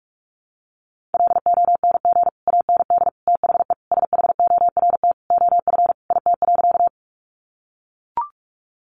Morse code, heard as a keyed tone, about 44 wpm, 700 Hz. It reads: LONG AND THE SHORT OF IT.
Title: Sentences from Top 400 Words 45wpm